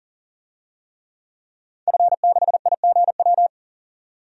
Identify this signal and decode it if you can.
Morse code, heard as a keyed tone, about 39 wpm, 700 Hz. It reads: F6IGW
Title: DXCC-100 Call Signs - 5 Characters - 40wpm